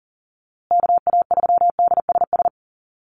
Morse code, heard as a keyed tone, about 40 wpm, 700 Hz. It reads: KA3DSS
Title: USA Call Signs Repeated 3x 40wpm